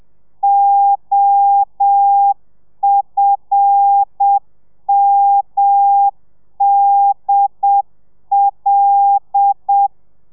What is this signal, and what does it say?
Morse code, heard as a keyed tone, about 7 wpm, 800 Hz. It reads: OFMDL
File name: telegrafia / i2tfj / Lez5 30